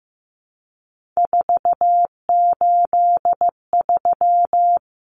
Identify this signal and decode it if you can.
Morse code, heard as a keyed tone, about 15 wpm, 700 Hz. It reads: 483
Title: Numbers - 3-Digit 15wpm